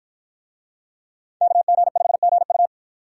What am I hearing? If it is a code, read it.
Morse code, heard as a keyed tone, about 53 wpm, 700 Hz. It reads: KG5GU